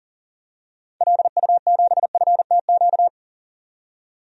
Morse code, heard as a keyed tone, about 40 wpm, 700 Hz. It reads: LU7FTQ
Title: DXCC-100 Call Signs - 6 Characters - 40wpm